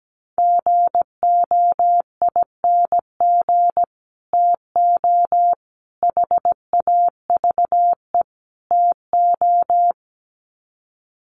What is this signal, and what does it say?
Morse code, heard as a keyed tone, about 17 wpm, 700 Hz. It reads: GOING TO HAVE TO
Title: Sets of 4 Words 17wpm